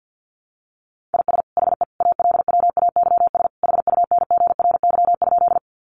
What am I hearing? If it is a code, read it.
Morse code, heard as a keyed tone, about 48 wpm, 700 Hz. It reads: IS HE ALWAYS HUNGRY?
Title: Sentences from Top 1400 Words 50wpm